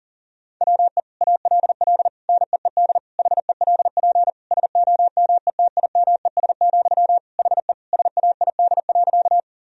Morse code, heard as a keyed tone, about 40 wpm, 700 Hz. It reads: WE ALL NEED HELP SOMETIMES, HE SAID.
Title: Sentences from Top 1000 Words - Review Part 3 - 40wpm